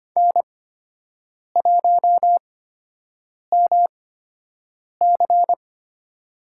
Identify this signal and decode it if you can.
Morse code, heard as a keyed tone, about 25 wpm, spaced as for 7 wpm, 700 Hz. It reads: N1MC
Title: USA Call Signs - 4 Characters - 8x Character Spacing - 25wpm